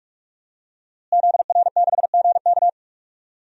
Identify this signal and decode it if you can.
Morse code, heard as a keyed tone, about 45 wpm, 700 Hz. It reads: ZR6GK
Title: DXCC-100 Call Signs - 5 Characters - Repeated 3x - 45wpm